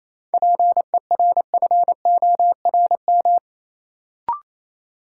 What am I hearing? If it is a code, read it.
Morse code, heard as a keyed tone, about 28 wpm, 700 Hz. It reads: PERFORM
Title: Sentences from Top 1000 Words - Review Part 5 - New Word Drill - 28wpm